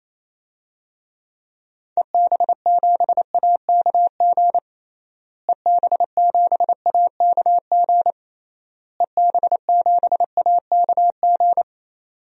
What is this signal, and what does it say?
Morse code, heard as a keyed tone, about 28 wpm, 700 Hz. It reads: EB7AKG EB7AKG EB7AKG
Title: DXCC-100 Call Signs - 6 Characters - Repeated 3x - 28wpm